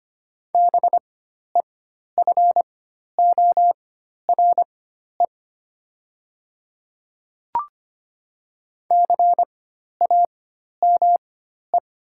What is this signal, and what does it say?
Morse code, heard as a keyed tone, about 25 wpm, spaced as for 12 wpm, 700 Hz. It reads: BEFORE CAME
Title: 4x Character Spacing - Top 200 Words - 25wpm